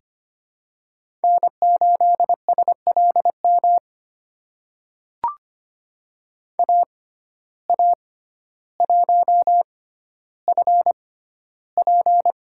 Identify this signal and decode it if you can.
Morse code, heard as a keyed tone, about 25 wpm, spaced as for 13 wpm, 700 Hz. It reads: N8SLM AA1FP
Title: USA Call Signs - 5 Characters - 6x Character Spacing - 25wpm